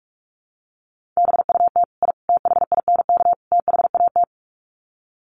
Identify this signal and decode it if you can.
Morse code, heard as a keyed tone, about 44 wpm, 700 Hz. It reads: BUT I THINK THAT
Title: Sets of 4 Words 45wpm